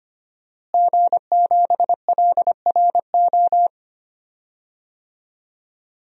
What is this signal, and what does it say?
Morse code, heard as a keyed tone, about 25 wpm, 700 Hz. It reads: G7LRO
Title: DXCC-100 Call Signs - 5 Characters - Repeated 3x - 25wpm